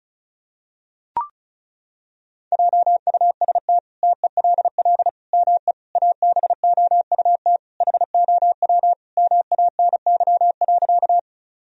Morse code, heard as a keyed tone, about 34 wpm, 700 Hz. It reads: JUST TELL ME ABOUT HOW MANY.
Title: Sentences from Top 1000 Words - Review Part 1 - 35wpm